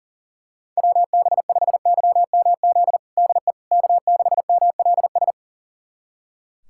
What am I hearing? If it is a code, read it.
Morse code, heard as a keyed tone, about 40 wpm, 700 Hz. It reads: WB5YMZ DE K6MLS
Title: USA Callsign DE Callsign 40wpm